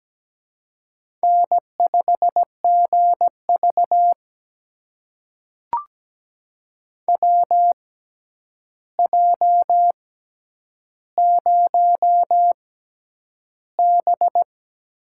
Morse code, heard as a keyed tone, about 17 wpm, spaced as for 9 wpm, 700 Hz. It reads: N5GV WJ0B
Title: USA Call Signs - 4 Characters - 6x Character Spacing - 17wpm